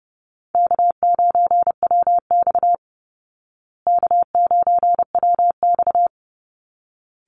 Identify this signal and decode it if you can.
Morse code, heard as a keyed tone, about 30 wpm, 700 Hz. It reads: K9WX K9WX
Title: Call Signs Repeated 3x - 30wpm